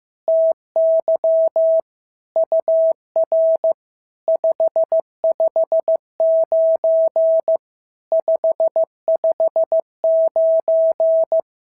Morse code, heard as a keyed tone, about 15 wpm, 650 Hz. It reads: TY UR 559 559
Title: SOTA QSO Practice 15wpm